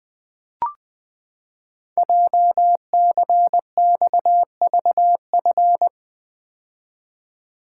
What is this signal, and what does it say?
Morse code, heard as a keyed tone, about 20 wpm, 700 Hz. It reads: JCXVF
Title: Code Groups - 5 Letters - 20wpm